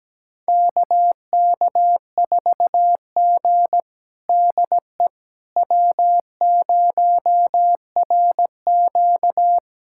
Morse code, heard as a keyed tone, about 17 wpm, 700 Hz. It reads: KK4G DE W0RQ
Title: USA Callsign DE Callsign - 4 Characters - 17wpm